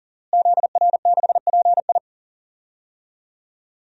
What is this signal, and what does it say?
Morse code, heard as a keyed tone, about 38 wpm, 700 Hz. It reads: ZR6PI
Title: DXCC-100 Call Signs - 5 Characters - Repeated 2x - 40wpm